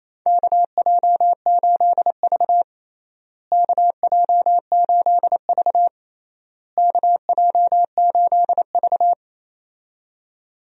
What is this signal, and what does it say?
Morse code, heard as a keyed tone, about 28 wpm, 700 Hz. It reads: KJ8V KJ8V KJ8V
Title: USA Call Signs - 4 Characters - Repeated 3x - 28wpm